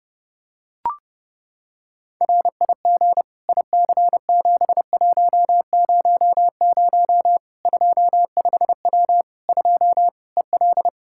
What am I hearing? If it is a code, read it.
Morse code, heard as a keyed tone, about 30 wpm, 700 Hz. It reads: RIG IC7100 25W 2 EL
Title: K1USN SST QSO Practice 30wpm